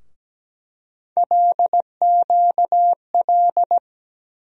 Morse code, heard as a keyed tone, about 17 wpm, 700 Hz. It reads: LQL
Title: USA Call Sign Postfixes - 3 Letters 17wpm